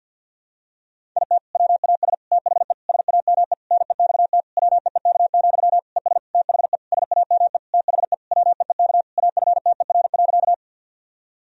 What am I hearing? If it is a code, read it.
Morse code, heard as a keyed tone, about 53 wpm, 700 Hz. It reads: IT WAS THE SAME NEXT WEEK, ES THE SAME THE WEEK AFTER.
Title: Sentences from Top 500 Words and Abbreviations 50wpm